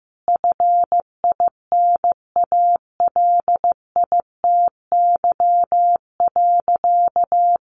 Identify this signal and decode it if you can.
Morse code, heard as a keyed tone, about 15 wpm, 700 Hz. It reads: FINALITY.
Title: A Sound Of Thunder 15wpm